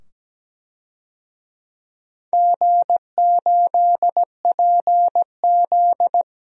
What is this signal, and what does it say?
Morse code, heard as a keyed tone, about 17 wpm, 700 Hz. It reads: G8PZ
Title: DXCC-100 Call Signs - 4 Characters - 17wpm